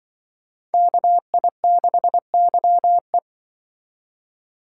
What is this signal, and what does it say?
Morse code, heard as a keyed tone, about 25 wpm, 700 Hz. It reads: KI6YE
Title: Speed Racing - USA 2x2 Call Signs 20wpm